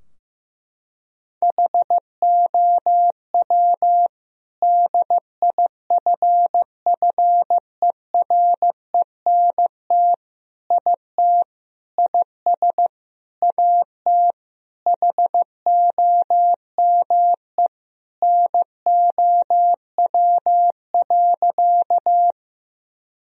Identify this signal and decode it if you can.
Morse code, heard as a keyed tone, about 15 wpm, 700 Hz. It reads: HOW DIFFERENT IT IS AT HOME NOW.
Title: Sentences from Top 1000 Words - Review Part 3 - 15wpm